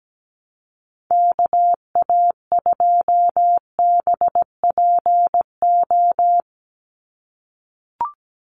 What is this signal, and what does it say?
Morse code, heard as a keyed tone, about 17 wpm, 700 Hz. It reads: KA2BPO
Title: USA Call Signs 17wpm